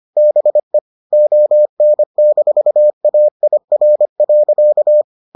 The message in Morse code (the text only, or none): BE ON-AIR.